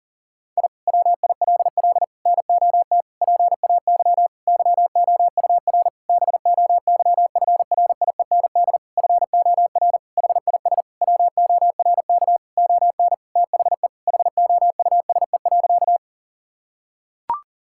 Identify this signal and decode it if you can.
Morse code, heard as a keyed tone, about 40 wpm, 700 Hz. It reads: I WILL NOT PAY YOUR BOYFRIEND FOR HIS WORK ON THE HOUSE.
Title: Sentences from Top 1000 Words - Review Part 5 - 40wpm